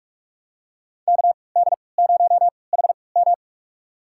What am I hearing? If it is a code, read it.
Morse code, heard as a keyed tone, about 46 wpm, spaced as for 25 wpm, 700 Hz. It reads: KD0HM
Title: USA Call Signs - 5 Characters - 3x Character Spacing - 45wpm